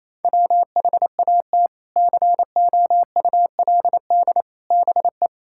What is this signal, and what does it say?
Morse code, heard as a keyed tone, about 28 wpm, 700 Hz. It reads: WHAT COULD BE
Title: Sentences from Top 1000 Words - Review Part 6 - 28wpm